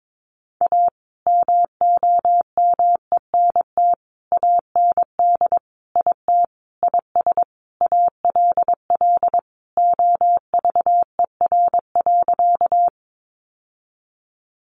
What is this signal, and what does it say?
Morse code, heard as a keyed tone, about 22 wpm, 700 Hz. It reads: A MOMENT AND IT IS ALL OVER.
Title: Sentences from Top 1000 Words 22wpm